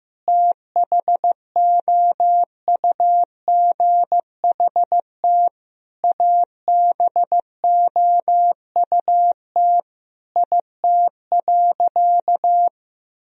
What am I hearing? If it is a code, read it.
Morse code, heard as a keyed tone, about 15 wpm, 700 Hz. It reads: THOUGHT ABOUT IT.
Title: Sentences from Top 1000 Words - Review Part 1 - 15wpm